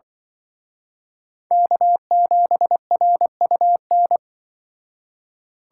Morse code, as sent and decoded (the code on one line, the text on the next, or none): -.- --... .-. ..- -.
K7RUN